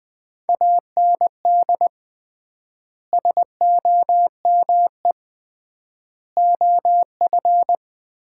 Morse code, heard as a keyed tone, about 20 wpm, 700 Hz. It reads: AND SOME OF
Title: 3x Word-Spacing - Sets of 4 Words - 20wpm